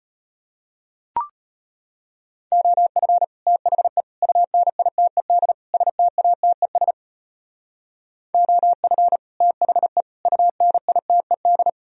Morse code, heard as a keyed tone, about 33 wpm, 700 Hz. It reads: OF THE UNITED STATES OF THE UNITED